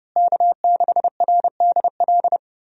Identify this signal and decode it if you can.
Morse code, heard as a keyed tone, about 32 wpm, 700 Hz. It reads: K6RDL
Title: USA Call Signs - 5 Characters - Repeated 3x - 30wpm